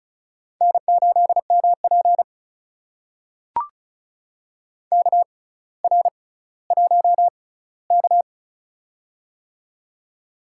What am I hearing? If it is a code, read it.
Morse code, heard as a keyed tone, about 36 wpm, spaced as for 18 wpm, 700 Hz. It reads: N8MP KR1K
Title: USA Call Signs - 4 Characters - 6x Character Spacing - 35wpm